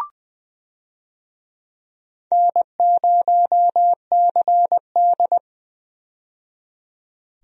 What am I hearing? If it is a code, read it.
Morse code, heard as a keyed tone, about 20 wpm, 700 Hz. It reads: N0CD
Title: USA Call Signs - 4 Characters - 20wpm